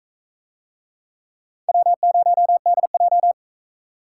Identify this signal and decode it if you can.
Morse code, heard as a keyed tone, about 44 wpm, 700 Hz. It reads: W0DJ